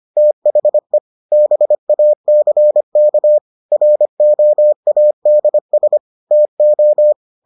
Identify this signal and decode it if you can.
Morse code, heard as a keyed tone, about 25 wpm, 600 Hz. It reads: THE BACK ROADS TO